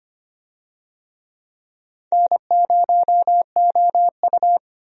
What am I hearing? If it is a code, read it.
Morse code, heard as a keyed tone, about 25 wpm, 700 Hz. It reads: N0OU
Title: USA Call Signs - 4 Characters - Repeated 3x - 25wpm